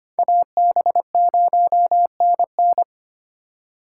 Morse code, heard as a keyed tone, about 32 wpm, 700 Hz. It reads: AB0NN